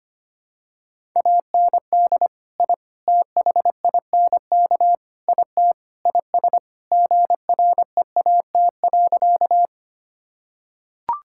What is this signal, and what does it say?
Morse code, heard as a keyed tone, about 25 wpm, 700 Hz. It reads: AND I THINK IT IS GREAT.
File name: practice / 25wpm / Sentences with 4-Word Phrases - 25wpm